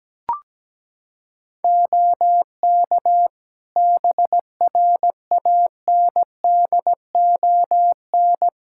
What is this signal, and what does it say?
Morse code, heard as a keyed tone, about 17 wpm, 700 Hz. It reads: OK BRANDON
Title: K1USN SST QSO Practice 17wpm